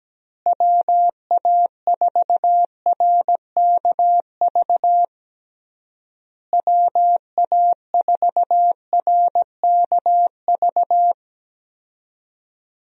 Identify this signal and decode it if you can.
Morse code, heard as a keyed tone, about 17 wpm, 700 Hz. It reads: WA4RKV WA4RKV
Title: USA Call Signs - 6 Characters - Repeated 3x - 17wpm